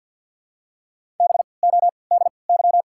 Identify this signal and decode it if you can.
Morse code, heard as a keyed tone, about 53 wpm, spaced as for 29 wpm, 690 Hz. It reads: BODY
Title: 3x Character Spacing - Top 500 Words - 50wpm